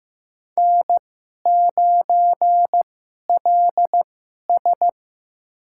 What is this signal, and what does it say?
Morse code, heard as a keyed tone, about 15 wpm, spaced as for 11 wpm, 700 Hz. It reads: N9LS
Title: USA Call Signs - 2x Character Spacing - 15wpm